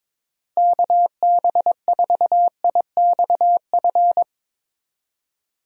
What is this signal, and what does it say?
Morse code, heard as a keyed tone, about 22 wpm, 700 Hz. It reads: KB4IXF
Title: USA Call Signs - 6 Characters - Repeated 3x - 22wpm